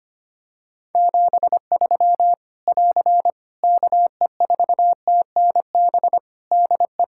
Morse code, heard as a keyed tone, about 25 wpm, 700 Hz. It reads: 73 <AR> KE4TNB DE
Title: Practice QSO - Exchange 4 - Ending - 25wpm